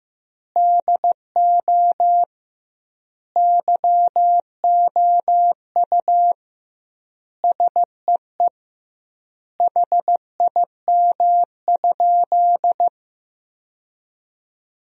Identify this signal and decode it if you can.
Morse code, heard as a keyed tone, about 15 wpm, 700 Hz. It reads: DO YOU SEE HIM?
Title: 2x Word-Spacing - Sentences from Top 100 Words - 15wpm